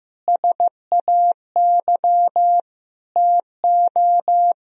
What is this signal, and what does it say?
Morse code, heard as a keyed tone, about 15 wpm, 700 Hz. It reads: SAY TO
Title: Sentences from Top 1000 Words - Review Part 1 - 15wpm